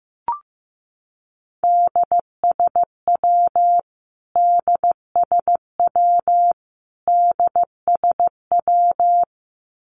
Morse code, heard as a keyed tone, about 14 wpm, 700 Hz. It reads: DSW DSW DSW
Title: Abbreviations Repeated 3x - 15wpm